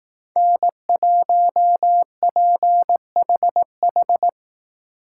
Morse code, heard as a keyed tone, about 18 wpm, 700 Hz. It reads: N1PHH
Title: Speed Racing - USA Call Signs 15wpm